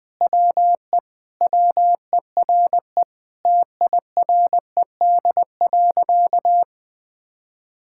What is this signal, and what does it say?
Morse code, heard as a keyed tone, about 21 wpm, 700 Hz. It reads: WE WERE TIRED.
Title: Sentences with 5-Word Phrases - 20wpm